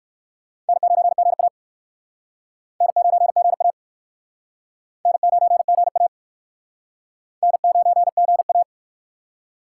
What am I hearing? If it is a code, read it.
Morse code, heard as a keyed tone, about 61 wpm, 700 Hz. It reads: N9GA N9GA N9GA N9GA